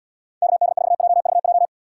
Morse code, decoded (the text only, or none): KD4JVJ